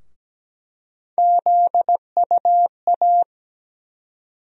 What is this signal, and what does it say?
Morse code, heard as a keyed tone, about 17 wpm, 700 Hz. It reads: ZUA
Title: USA Call Sign Postfixes - 3 Letters 17wpm